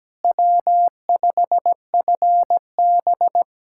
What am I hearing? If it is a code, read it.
Morse code, heard as a keyed tone, about 17 wpm, 700 Hz. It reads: W5FB
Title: USA Callsign DE Callsign - 4 Characters - 17wpm